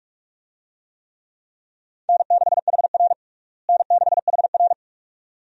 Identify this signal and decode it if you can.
Morse code, heard as a keyed tone, about 48 wpm, 700 Hz. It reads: N6HR N6HR
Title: USA Call Signs - 4 Characters - Repeated 2x - 45wpm